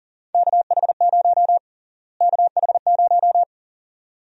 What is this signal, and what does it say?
Morse code, heard as a keyed tone, about 37 wpm, 700 Hz. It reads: KH0 KH0
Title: USA Call Sign Prefixes - Repeated 2x - 40wpm